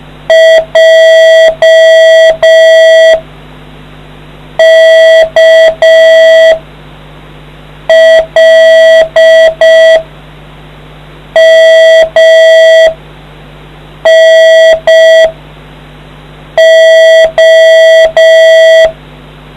Morse code, heard as a keyed tone, about 5 wpm, 650 Hz, with noise. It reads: J K L M N O